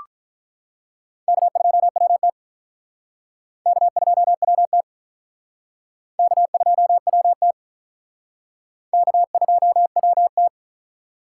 Morse code, heard as a keyed tone, about 46 wpm, 700 Hz. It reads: K2WT K2WT K2WT K2WT